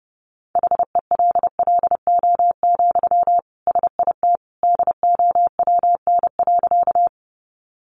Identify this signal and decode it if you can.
Morse code, heard as a keyed tone, about 30 wpm, 700 Hz. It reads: HELLO, SIT DOWN.
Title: Sentences from Top 1100 Words 30wpm